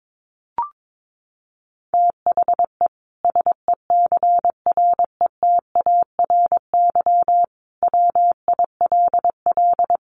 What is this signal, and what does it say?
Morse code, heard as a keyed tone, about 22 wpm, 700 Hz. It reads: THE SECRETARY WILL
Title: Sentences from Top 1400 Words 22wpm